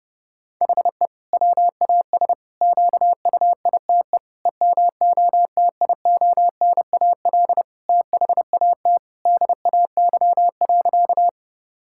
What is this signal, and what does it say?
Morse code, heard as a keyed tone, about 30 wpm, 700 Hz. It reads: HE WAS QUITE EMOTIONAL THAT DAY.
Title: Sentences from Top 1000 Words - Review Part 5 - 30wpm